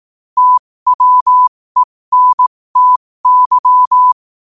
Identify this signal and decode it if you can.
Morse code, heard as a keyed tone, about 18 wpm, spaced as for 15 wpm, 1000 Hz. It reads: TWENTY
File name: qotdfiles / 15wpm